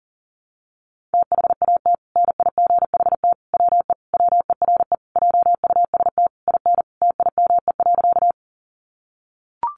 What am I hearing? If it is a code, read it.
Morse code, heard as a keyed tone, about 37 wpm, 700 Hz. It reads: THAT NIGHT WE WERE JUST IN TIME.
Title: Sentences from Top 300 Words 40wpm